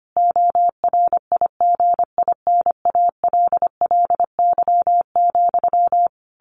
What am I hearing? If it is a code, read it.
Morse code, heard as a keyed tone, about 25 wpm, 700 Hz. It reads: ORIGINALLY,
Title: A Sound Of Thunder 25wpm